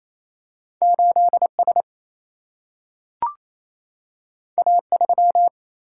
Morse code, heard as a keyed tone, about 28 wpm, 700 Hz. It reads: 8S A3
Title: Single-Letter Single-Number 28wpm